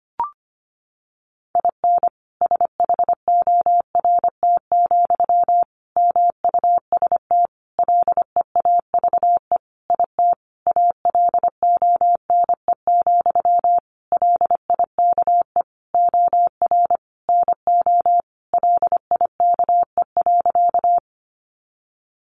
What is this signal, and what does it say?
Morse code, heard as a keyed tone, about 25 wpm, 700 Hz. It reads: IN SHORT, MUST LEAVE IT ALONE, LIKE OR NO LIKE.
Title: Sentences from Top 1000 Words 25wpm